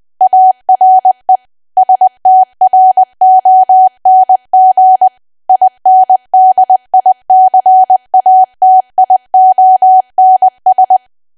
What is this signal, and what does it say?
Morse code, heard as a keyed tone, about 20 wpm, 750 Hz. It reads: ARE STRONG INDICATIONS
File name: cwa / W1AW-Level3 / 20WPM